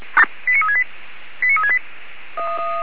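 Five-tone selective call (ZVEI), heard as quick runs of five tones, tones of about 70 ms. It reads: 89368 89368
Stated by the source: ZVEI five-tone selective call received on an SDR.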